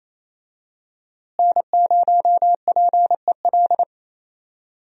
Morse code, heard as a keyed tone, about 27 wpm, 700 Hz. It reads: N0PEL
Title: USA Call Signs - 5 Characters - Repeated 3x - 28wpm